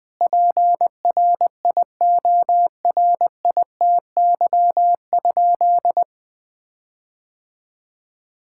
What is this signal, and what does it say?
Morse code, heard as a keyed tone, about 20 wpm, 700 Hz. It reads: PRIORITY?
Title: Sentences from Top 1600 Words 20wpm